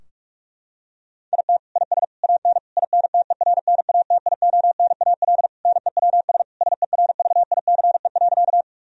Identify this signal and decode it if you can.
Morse code, heard as a keyed tone, about 46 wpm, 700 Hz. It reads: IT IS AN INTERNATIONAL NEWS SERVICE.
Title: Sentences from Top 1000 Words - Review Part 4 - 45wpm